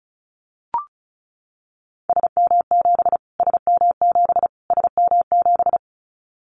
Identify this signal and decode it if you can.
Morse code, heard as a keyed tone, about 34 wpm, 700 Hz. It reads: SM7 SM7 SM7
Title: Callsign Prefixes for Top 25 DXCC - Repeated 3x - 35wpm